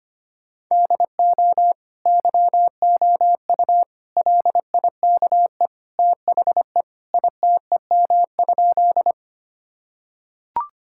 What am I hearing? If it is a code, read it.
Morse code, heard as a keyed tone, about 25 wpm, 700 Hz. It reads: DO YOU LIKE THE ITEM?
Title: Sentences from Top 1000 Words - Review Part 6 - 25wpm